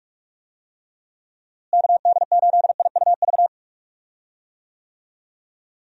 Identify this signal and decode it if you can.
Morse code, heard as a keyed tone, about 46 wpm, 700 Hz. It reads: KD8IUV
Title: USA Call Signs - 6 Characters - Repeated 2x - 45wpm